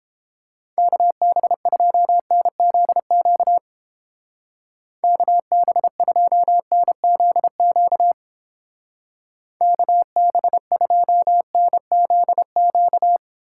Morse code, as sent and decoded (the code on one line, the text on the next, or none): -.- -... ..--- -. --.. --.- / -.- -... ..--- -. --.. --.- / -.- -... ..--- -. --.. --.-
KB2NZQ KB2NZQ KB2NZQ